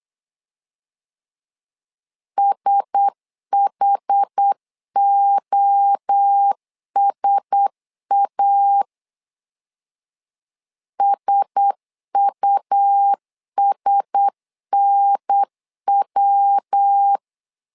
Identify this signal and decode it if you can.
Morse code, heard as a keyed tone, about 8 wpm, 800 Hz. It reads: SHOSA SUSNW
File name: LEZIONI / 4 Q1S 8wpm 2710